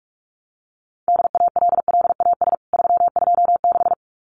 Morse code, heard as a keyed tone, about 44 wpm, 700 Hz. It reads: DALLAS 326
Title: CWT Exchange 45wpm